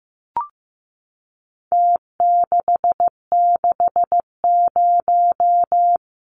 Morse code, heard as a keyed tone, about 15 wpm, 700 Hz. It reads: T660
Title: Numbers - Contest Style - 4-Digit 15wpm